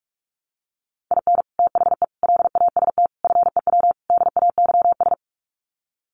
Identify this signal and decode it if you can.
Morse code, heard as a keyed tone, about 44 wpm, 700 Hz. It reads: IN THE LAST FEW DAYS
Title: Sets of 5 Words Encore2 45wpm